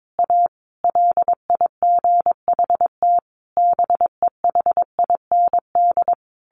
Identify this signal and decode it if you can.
Morse code, heard as a keyed tone, about 22 wpm, 700 Hz. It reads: A LIGHT BEHIND
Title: Sentences from Top 1000 Words 22wpm